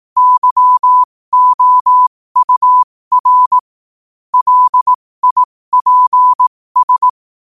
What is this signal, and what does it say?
Morse code, heard as a keyed tone, about 18 wpm, spaced as for 15 wpm, 1000 Hz. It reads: YOUR LIPS
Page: 15 WPM morse code quotes